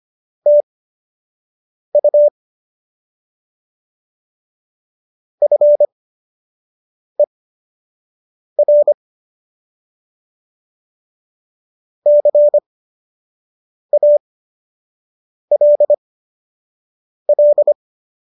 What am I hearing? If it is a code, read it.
Morse code, heard as a keyed tone, about 25 wpm, spaced as for 6 wpm, 590 Hz. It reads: TU FER CALL